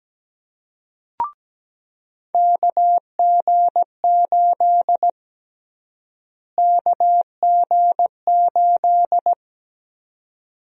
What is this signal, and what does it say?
Morse code, heard as a keyed tone, about 17 wpm, 700 Hz. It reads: KG8 KG8
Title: USA Call Sign Prefixes - Repeated 2x - 17wpm